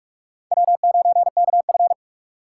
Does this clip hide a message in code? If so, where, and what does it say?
Yes: Morse code from 0.5 to 1.9 s, heard as a keyed tone, about 44 wpm, 700 Hz. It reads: W9KF